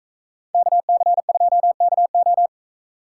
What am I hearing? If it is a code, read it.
Morse code, heard as a keyed tone, about 44 wpm, 700 Hz. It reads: KC2KO